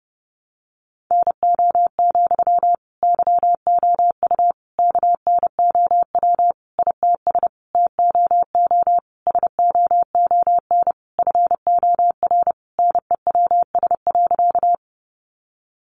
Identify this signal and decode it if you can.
Morse code, heard as a keyed tone, about 30 wpm, 700 Hz. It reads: NO, YOU KNOW ITS TOO SOON FOR NEWS.
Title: Sentences from Top 800 Words 30wpm